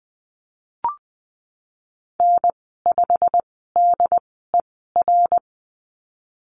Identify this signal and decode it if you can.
Morse code, heard as a keyed tone, about 20 wpm, spaced as for 15 wpm, 700 Hz. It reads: N5DER
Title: 088 Lesson 15g - Number 5 - Call Signs 20wpm